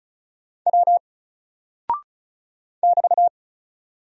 Morse code, heard as a keyed tone, about 36 wpm, 700 Hz. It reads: W <BT>
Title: Single Character (Letter:Number:Punctuation:Prosign) 35wpm